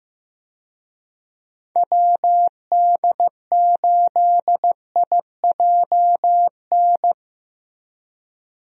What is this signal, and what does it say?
Morse code, heard as a keyed tone, about 15 wpm, 700 Hz. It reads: WD8IJN
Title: USA Call Signs - 6 Characters - Repeated 2x - 15wpm